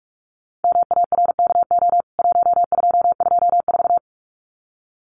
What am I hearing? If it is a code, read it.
Morse code, heard as a keyed tone, about 44 wpm, 700 Hz. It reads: MARKO 1224
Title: CWT Exchange 45wpm